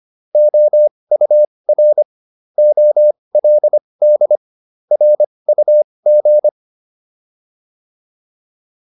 Morse code, heard as a keyed tone, about 25 wpm, spaced as for 20 wpm, 600 Hz. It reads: OUR OLD RUG